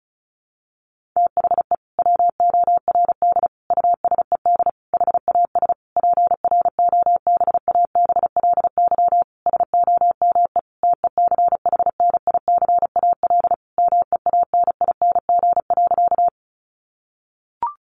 Morse code, heard as a keyed tone, about 34 wpm, 700 Hz. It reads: THE WORD USED HAS PROBABLY SOME TECHNICAL MEANING.
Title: Sentences from Top 1100 Words 35wpm